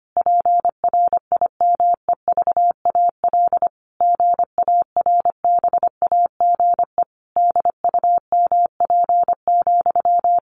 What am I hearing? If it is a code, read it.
Morse code, heard as a keyed tone, about 25 wpm, 700 Hz. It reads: PRIMEVAL GARBAGE DUMP,
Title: A Sound Of Thunder 25wpm